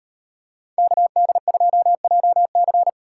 Morse code, heard as a keyed tone, about 35 wpm, 700 Hz. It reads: KD2JC